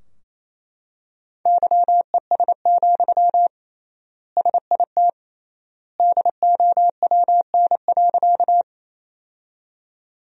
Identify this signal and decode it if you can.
Morse code, heard as a keyed tone, about 28 wpm, 700 Hz. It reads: YES, SIT DOWN.
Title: 3x Word-Spacing - Sentences from Top 500 Words - 28wpm